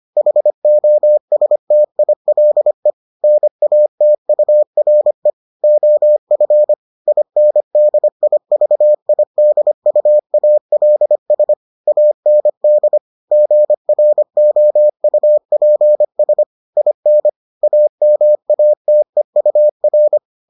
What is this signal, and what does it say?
Morse code, heard as a keyed tone, about 25 wpm, 600 Hz. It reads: HOSTILE NATURE OF INDIVIDUALS AND GROUPS IN AMATEUR